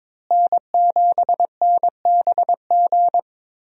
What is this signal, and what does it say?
Morse code, heard as a keyed tone, about 22 wpm, 700 Hz. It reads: N7NBG
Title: USA Call Signs - 5 Characters - Repeated 2x - 22wpm